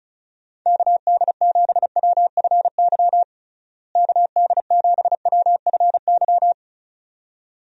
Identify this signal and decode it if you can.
Morse code, heard as a keyed tone, about 34 wpm, 700 Hz. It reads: KD7WFY KD7WFY
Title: USA Call Signs - 6 Characters - Repeated 2x - 35wpm